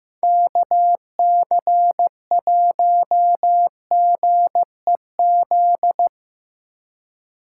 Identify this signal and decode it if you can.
Morse code, heard as a keyed tone, about 15 wpm, 700 Hz. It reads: KC1GEZ
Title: USA Call Signs - 4x Character Spacing - 15wpm